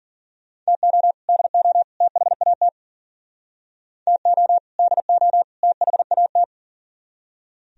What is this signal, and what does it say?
Morse code, heard as a keyed tone, about 50 wpm, 700 Hz. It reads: TO DO THAT TO DO THAT